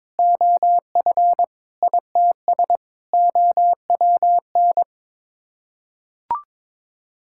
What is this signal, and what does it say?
Morse code, heard as a keyed tone, about 22 wpm, 700 Hz. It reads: OF ITS OWN
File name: practice / 22wpm / Sets of 8 Words - 22wpm